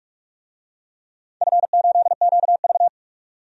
Morse code, heard as a keyed tone, about 46 wpm, 700 Hz. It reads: F8QV
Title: DXCC-100 Call Signs - 4 Characters - 45wpm